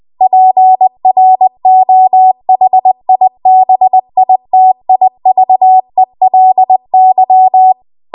Morse code, heard as a keyed tone, about 20 wpm, 750 Hz. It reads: PROHIBITIVELY
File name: Archive / 20 WPM / 20WPM